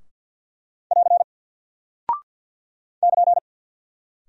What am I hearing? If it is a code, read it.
Morse code, heard as a keyed tone, about 48 wpm, 700 Hz. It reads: <AR> <KN>
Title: Single Character (Letter:Number:Punctuation:Prosign) 50wpm